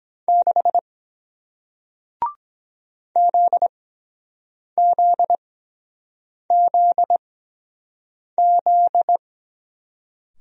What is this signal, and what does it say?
Morse code, heard as a keyed tone, about 23 wpm, spaced as for 7 wpm, 700 Hz. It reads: 6 ZZZZ